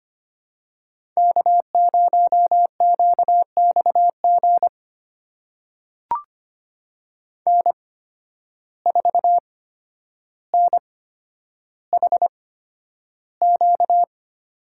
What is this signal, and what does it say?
Morse code, heard as a keyed tone, about 25 wpm, spaced as for 11 wpm, 700 Hz. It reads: K0QXG N4NHQ
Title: USA Call Signs - 8x Character Spacing - 25wpm